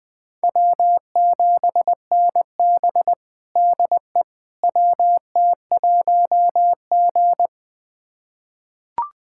Morse code, heard as a keyed tone, about 21 wpm, 700 Hz. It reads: W7NB DE WT1G
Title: USA Callsign DE Callsign - 4 Characters - 20wpm